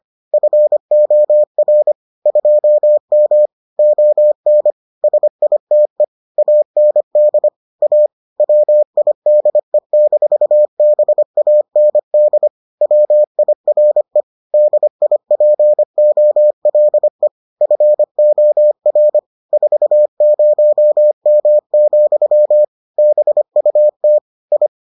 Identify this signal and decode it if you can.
Morse code, heard as a keyed tone, about 25 wpm, 600 Hz. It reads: FOR 2M ON SITE AND A WIDE-BAND WIRE DIPOLE FOR 40M, BUT I